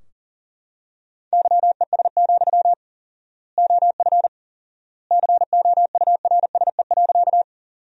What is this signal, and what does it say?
Morse code, heard as a keyed tone, about 40 wpm, 700 Hz. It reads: YES, OF COURSE.
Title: 4x Word-Spacing - Sentences from Top 500 Words - 40wpm